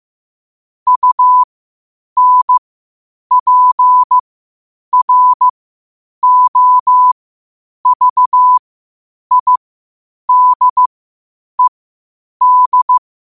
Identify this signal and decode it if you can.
Morse code, heard as a keyed tone, about 15 wpm, spaced as for 8 wpm, 1000 Hz. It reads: UNPROVIDED
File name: qotdfiles / 8wpm